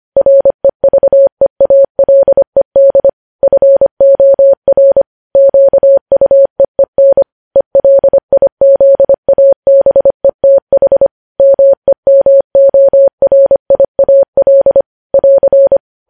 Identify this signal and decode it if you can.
Morse code, heard as a keyed tone, about 25 wpm, 550 Hz. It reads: REVEALED FOR QUEEN ELIZABETH MEMORIAL <AR>
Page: news headlines in morse code at 25 wpm